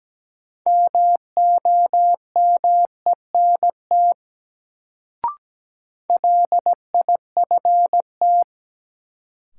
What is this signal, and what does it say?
Morse code, heard as a keyed tone, about 17 wpm, 700 Hz. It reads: MOMENT LIFT
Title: Sentences from Top 1000 Words - New Word Drill - 17wpm